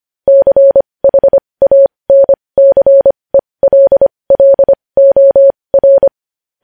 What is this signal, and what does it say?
Morse code, heard as a keyed tone, about 25 wpm, spaced as for 20 wpm, 550 Hz. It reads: CHANCELLOR
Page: news-headlines-in-morse-code-at-20-wpm